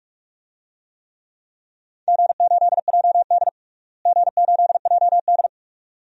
Morse code, heard as a keyed tone, about 46 wpm, 700 Hz. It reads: G8JD G8JD
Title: DXCC-100 Call Signs - 4 Characters - Repeated 2x - 45wpm